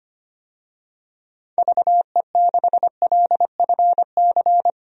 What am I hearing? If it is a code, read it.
Morse code, heard as a keyed tone, about 25 wpm, 700 Hz. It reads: VE6LFC
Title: DXCC-100 Call Signs - 6 Characters - 25wpm